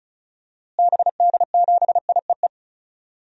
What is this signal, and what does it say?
Morse code, heard as a keyed tone, about 34 wpm, 700 Hz. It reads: BD7IEE
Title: DXCC-100 Call Signs - 6 Characters - Repeated 2x - 35wpm